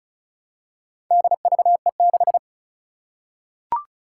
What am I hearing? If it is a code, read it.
Morse code, heard as a keyed tone, about 36 wpm, 700 Hz. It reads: DVE6
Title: Four Challenging Letter-Number Combinations 35wpm